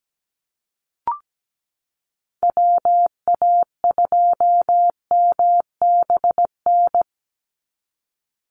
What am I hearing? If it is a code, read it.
Morse code, heard as a keyed tone, about 17 wpm, 700 Hz. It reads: WA2MBN
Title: USA Call Signs 17wpm